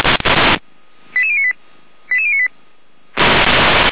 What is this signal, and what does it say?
Five-tone selective call (ZVEI), heard as quick runs of five tones, tones of about 70 ms. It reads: 88098 88098
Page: SDR/ZVEI